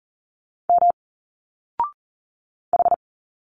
Morse code, heard as a keyed tone, about 40 wpm, 700 Hz. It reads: M H
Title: Single Letters 40wpm